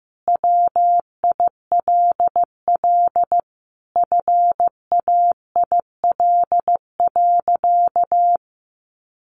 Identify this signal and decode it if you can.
Morse code, heard as a keyed tone, about 15 wpm, 700 Hz. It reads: WILL FAIL.
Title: Sentences from Top 1500 Words 15wpm